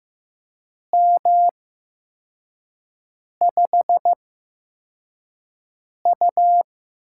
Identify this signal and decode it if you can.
Morse code, heard as a keyed tone, about 15 wpm, spaced as for 4 wpm, 700 Hz. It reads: M5U
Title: USA Call Signs - 4 Characters - 8x Character Spacing - 15wpm